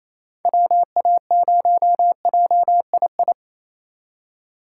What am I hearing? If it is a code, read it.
Morse code, heard as a keyed tone, about 33 wpm, 700 Hz. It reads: WA0JII
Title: Speed Racing - USA Call Signs 28wpm